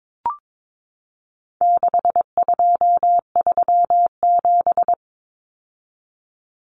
Morse code, heard as a keyed tone, about 22 wpm, 700 Hz. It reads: 6237
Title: Numbers - 4-Digit 22wpm